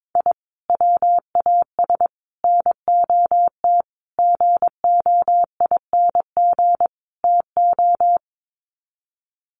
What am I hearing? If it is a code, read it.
Morse code, heard as a keyed tone, about 22 wpm, 700 Hz. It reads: I WAS NOT GOING TO
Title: Sets of 5 Words 22wpm